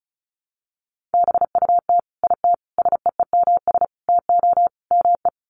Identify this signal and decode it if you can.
Morse code, heard as a keyed tone, about 34 wpm, 700 Hz. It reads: BUT IT SEEMS TO ME
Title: Sets of 5 Words Encore2 35wpm